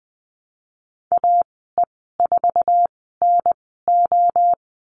Morse code, heard as a keyed tone, about 20 wpm, spaced as for 15 wpm, 700 Hz. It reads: AE4NO
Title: 062 Lesson 11g - Letter h - Call Signs 20wpm